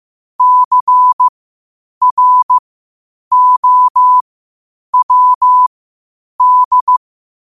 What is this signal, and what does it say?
Morse code, heard as a keyed tone, about 15 wpm, spaced as for 8 wpm, 1000 Hz. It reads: CROWD
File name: qotdfiles / 8wpm